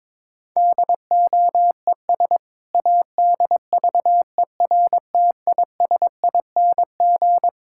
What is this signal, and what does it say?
Morse code, heard as a keyed tone, about 22 wpm, 700 Hz. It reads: DOES ADVERTISING
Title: Sentences from Top 1000 Words - Review Part 6 - 22wpm